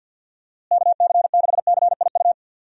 Morse code, heard as a keyed tone, about 50 wpm, 700 Hz. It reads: KC6CIU
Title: USA Call Signs - 6 Characters - Repeated 3x - 50wpm